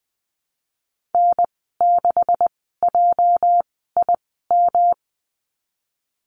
Morse code, heard as a keyed tone, about 20 wpm, spaced as for 15 wpm, 700 Hz. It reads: N6JIM
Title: 213 Lesson 34g - Number 8 - Call Signs 20wpm